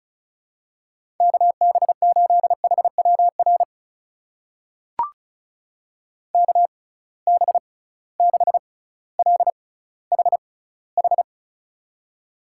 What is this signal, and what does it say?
Morse code, heard as a keyed tone, about 36 wpm, spaced as for 18 wpm, 700 Hz. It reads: KB8HWR KB6LHH
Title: USA Call Signs - 6 Characters - 6x Character Spacing - 35wpm